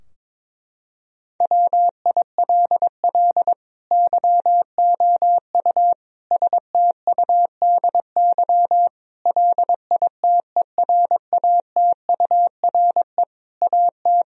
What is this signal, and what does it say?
Morse code, heard as a keyed tone, about 22 wpm, 700 Hz. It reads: WILL YOU STUDY LITERATURE AT
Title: Sentences from Top 1000 Words - Review Part 3 - 22wpm